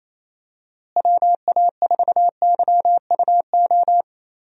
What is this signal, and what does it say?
Morse code, heard as a keyed tone, about 28 wpm, 700 Hz. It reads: WA4YUO
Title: USA Call Signs - 6 Characters - Repeated 2x - 28wpm